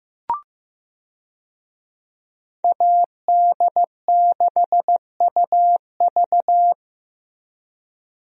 Morse code, heard as a keyed tone, about 15 wpm, 700 Hz. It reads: AD6UV
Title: USA Call Signs - 5 Characters - Repeated 3x - 15wpm